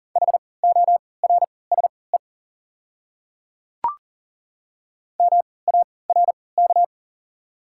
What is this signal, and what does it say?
Morse code, heard as a keyed tone, about 40 wpm, spaced as for 23 wpm, 700 Hz. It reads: HORSE MARK
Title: 3x Character Spacing - Top 300 Words - 40wpm